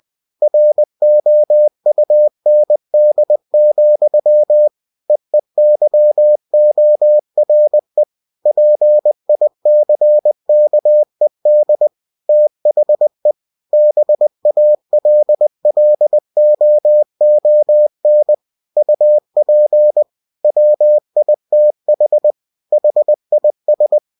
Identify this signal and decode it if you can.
Morse code, heard as a keyed tone, about 20 wpm, 600 Hz. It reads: ROUND, EEYORE PICKED THE BALLOON UP WITH HIS
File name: cw-books / winnie-the-pooh / winnie-20wpm-page-0058